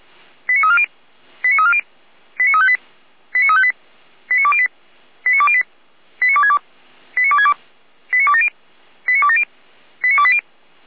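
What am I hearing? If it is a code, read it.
Five-tone selective call (ZVEI), heard as quick runs of five tones, tones of about 80 ms. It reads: 89340 89340 89358 89358 89208 89208 89262 89262 89270 89270 89280
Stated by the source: ZVEI five-tone selective call received on SDR.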